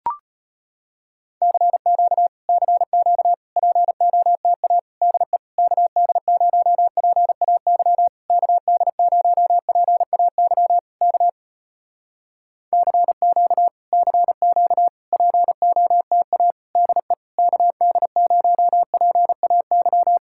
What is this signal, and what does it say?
Morse code, heard as a keyed tone, about 34 wpm, 700 Hz. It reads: CQ CQ POTA DE KD0PAY KD0PAY K CQ CQ POTA DE KD0PAY